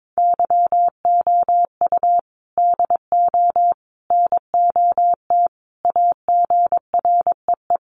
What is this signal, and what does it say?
Morse code, heard as a keyed tone, about 22 wpm, 700 Hz. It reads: YOU DO NOT AGREE
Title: Sentences from Top 1100 Words 22wpm